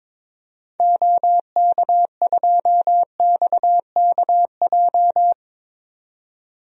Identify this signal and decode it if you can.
Morse code, heard as a keyed tone, about 22 wpm, 700 Hz. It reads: OK2XKJ
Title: DXCC-100 Call Signs - 6 Characters - 22wpm